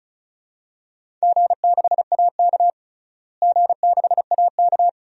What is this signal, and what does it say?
Morse code, heard as a keyed tone, about 34 wpm, 700 Hz. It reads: G6AK G6AK
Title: DXCC-100 Call Signs - 4 Characters - Repeated 3x - 35wpm